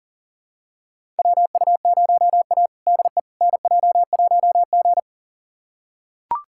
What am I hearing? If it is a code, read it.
Morse code, heard as a keyed tone, about 40 wpm, 700 Hz. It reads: WU0A DE NJ1G
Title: USA Callsign DE Callsign - 4 Characters - 40wpm